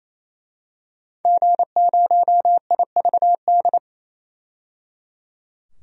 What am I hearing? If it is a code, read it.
Morse code, heard as a keyed tone, about 28 wpm, 700 Hz. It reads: G0IVD
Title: DXCC-100 Call Signs - 5 Characters - 28wpm